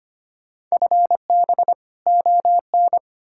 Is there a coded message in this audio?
Yes: Morse code from 0.7 to 3.0 s, heard as a keyed tone, about 33 wpm, 700 Hz. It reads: FB ON